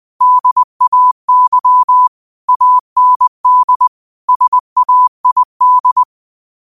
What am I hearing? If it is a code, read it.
Morse code, heard as a keyed tone, about 20 wpm, 1000 Hz. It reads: DAY AND SAID